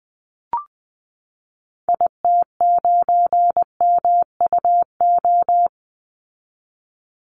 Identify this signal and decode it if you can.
Morse code, heard as a keyed tone, about 20 wpm, 700 Hz. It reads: IT9MUO
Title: Call Signs Repeated 2x - 20wpm